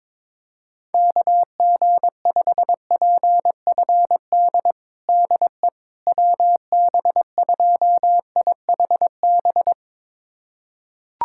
Morse code, heard as a keyed tone, about 22 wpm, 700 Hz. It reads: KG5PFD DE WB2IHB
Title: USA Callsign DE Callsign 22wpm